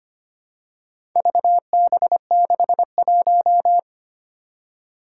Morse code, heard as a keyed tone, about 25 wpm, 700 Hz. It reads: VB61
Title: Four Challenging Letter-Number Combinations 25wpm